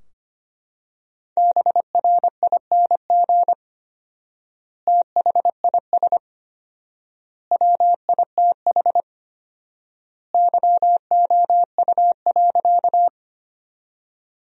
Morse code, heard as a keyed tone, about 25 wpm, 700 Hz. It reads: BRING THIS WITH YOU.
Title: 4x Word-Spacing - Sentences from Top 500 Words - 25wpm